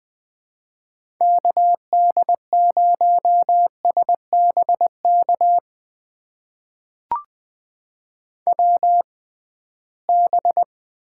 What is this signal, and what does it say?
Morse code, heard as a keyed tone, about 20 wpm, spaced as for 15 wpm, 700 Hz. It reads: KD0SBK WB
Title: USA Call Signs - 6 Characters - 6x Character Spacing - 20wpm